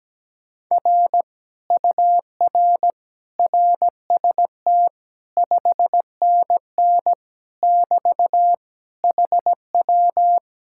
Morse code, heard as a keyed tone, about 17 wpm, 700 Hz. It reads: R UR RST 5NN <BT> HW
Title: Speed-Racing Common QSO Phrases - 17wpm